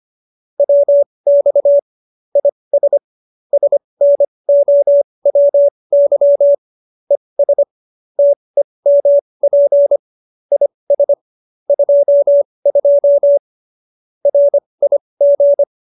Morse code, heard as a keyed tone, about 25 wpm, spaced as for 20 wpm, 570 Hz. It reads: WX IS SNOWY ES TEMP IS 22 RIG